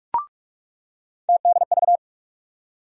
Morse code, heard as a keyed tone, about 42 wpm, 700 Hz. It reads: TDV
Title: USA Call Sign Postfixes - 3 Letters 45wpm